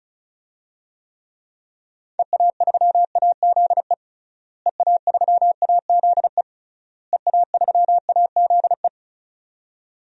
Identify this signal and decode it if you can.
Morse code, heard as a keyed tone, about 36 wpm, 700 Hz. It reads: EA3AZE EA3AZE EA3AZE
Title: DXCC-100 Call Signs - 6 Characters - Repeated 3x - 35wpm